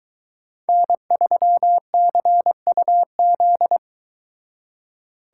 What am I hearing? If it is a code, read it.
Morse code, heard as a keyed tone, about 22 wpm, 700 Hz. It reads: N3CUZ